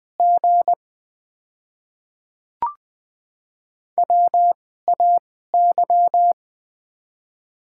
Morse code, heard as a keyed tone, about 20 wpm, spaced as for 14 wpm, 700 Hz. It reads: G WAY
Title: 2x Character Spacing - Top 500 Words - 20wpm